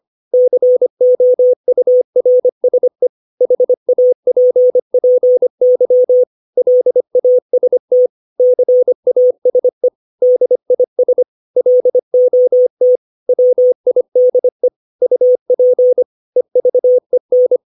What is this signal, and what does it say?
Morse code, heard as a keyed tone, about 25 wpm, 500 Hz. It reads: COURSE HAPPY LAST CASE DIS LOT WIDE UP EVEN
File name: 632 words / 632 Words 25WPM